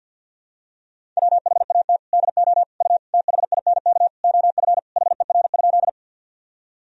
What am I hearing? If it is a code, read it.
Morse code, heard as a keyed tone, about 52 wpm, 700 Hz. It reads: WHAT DO U THINK OF HER?
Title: Sentences from Top 200 Words and Abbreviations 50wpm